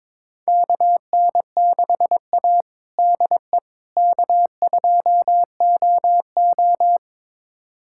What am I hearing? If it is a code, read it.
Morse code, heard as a keyed tone, about 22 wpm, 700 Hz. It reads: KN6A DE K2OO
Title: USA Callsign DE Callsign - 4 Characters - 22wpm